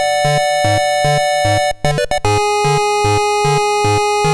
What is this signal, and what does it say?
There is no code.